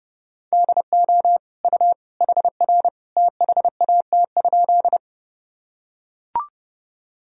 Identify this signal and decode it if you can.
Morse code, heard as a keyed tone, about 30 wpm, 700 Hz. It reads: DO U HR THAT?
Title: Sentences from Top 300 Words and Abbreviations 30wpm